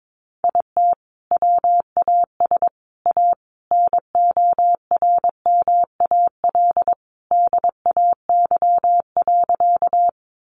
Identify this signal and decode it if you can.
Morse code, heard as a keyed tone, about 22 wpm, 700 Hz. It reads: IT WAS A NORMAL DAY.
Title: Sentences from Top 1200 Words 22wpm